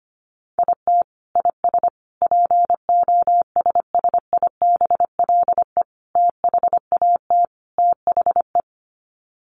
Morse code, heard as a keyed tone, about 25 wpm, 700 Hz. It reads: IT IS POSSIBLE THAT THE
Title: Sets of 5 Words Encore2 25wpm